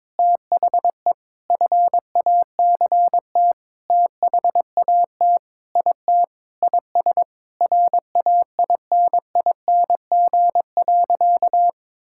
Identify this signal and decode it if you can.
Morse code, heard as a keyed tone, about 21 wpm, 700 Hz. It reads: THE FACT THAT IT IS RAINING.
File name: practice / 22wpm / Sentences with 8-Word Phrases - 22wpm